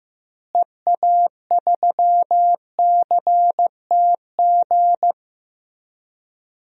Morse code, heard as a keyed tone, about 15 wpm, 700 Hz. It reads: EA3CTG
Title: DXCC-100 Call Signs - 6 Characters - Repeated 3x - 15wpm